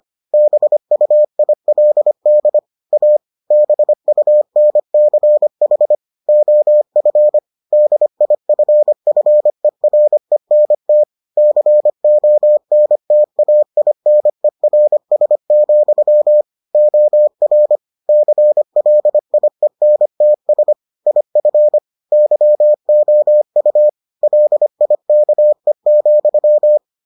Morse code, heard as a keyed tone, about 25 wpm, 600 Hz. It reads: BUILD A BUNCH OF DIFFERENT CONTAINERS, OR CLIENTS IF YOU LIKE,